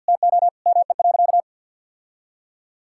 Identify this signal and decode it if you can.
Morse code, heard as a keyed tone, about 52 wpm, 700 Hz. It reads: TO ME.
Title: Sentences from Top 1000 Words - Review Part 6 - 50wpm